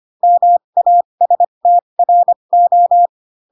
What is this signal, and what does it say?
Morse code, heard as a keyed tone, about 25 wpm, spaced as for 22 wpm, 700 Hz. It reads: MASTRO
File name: Pinocchio / Pinocchio ch2 18 25 ik1hge